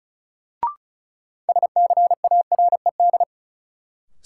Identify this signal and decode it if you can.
Morse code, heard as a keyed tone, about 34 wpm, 700 Hz. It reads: SCARED
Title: Sentences from Top 700 Words - New Word Drill - 35wpm